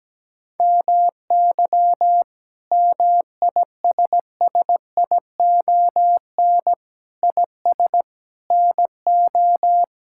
Morse code, heard as a keyed tone, about 17 wpm, 700 Hz. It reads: MY MISSION IS NO
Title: Sentences from Top 1600 Words 17wpm